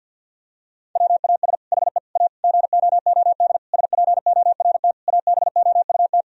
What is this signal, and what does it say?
Morse code, heard as a keyed tone, about 50 wpm, 700 Hz. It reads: WAS HE A GOOD SPORT ABOUT
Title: Sentences from Top 1000 Words - Review Part 4 - 50wpm